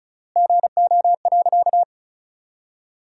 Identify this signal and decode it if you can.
Morse code, heard as a keyed tone, about 36 wpm, 700 Hz. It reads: GO.
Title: 3x Word-Spacing - Sentences from Top 300 Words - 35wpm